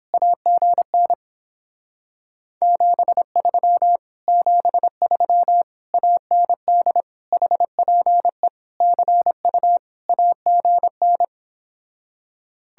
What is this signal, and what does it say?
Morse code, heard as a keyed tone, about 29 wpm, 700 Hz. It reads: AGN 73 73 AND HPE CU AGN